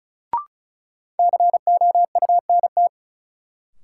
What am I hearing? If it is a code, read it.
Morse code, heard as a keyed tone, about 36 wpm, 700 Hz. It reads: COUNT
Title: Sentences from Top 1500 Words - New Word Drill - 35wpm